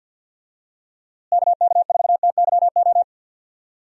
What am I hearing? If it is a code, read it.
Morse code, heard as a keyed tone, about 50 wpm, 700 Hz. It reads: KK4TYO
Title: USA Call Signs - 6 Characters - Repeated 3x - 50wpm